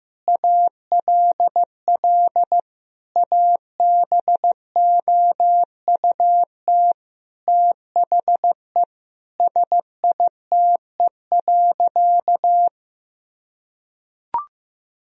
Morse code, heard as a keyed tone, about 16 wpm, 700 Hz. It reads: ALL ABOUT THE SITE.
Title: Sentences from Top 1000 Words - Review Part 4 - 15wpm